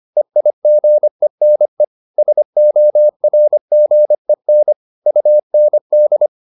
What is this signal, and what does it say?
Morse code, heard as a keyed tone, about 25 wpm, 600 Hz. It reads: EIGENE SORGEN UND